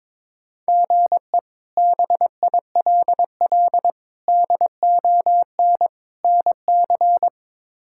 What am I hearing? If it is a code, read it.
Morse code, heard as a keyed tone, about 21 wpm, 700 Hz. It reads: GE BILL DON NC
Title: K1USN SST QSO Practice - 22wpm